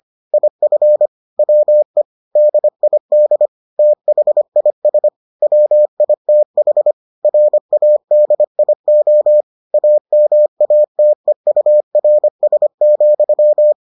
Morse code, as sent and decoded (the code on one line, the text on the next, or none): .. ..-. / .-- . / -.. .. -.. / - .... .. ... / .-- .. - .... / .-. .- -.. .. --- / .- -- .- - . ..- .-. ... --..--
IF WE DID THIS WITH RADIO AMATEURS,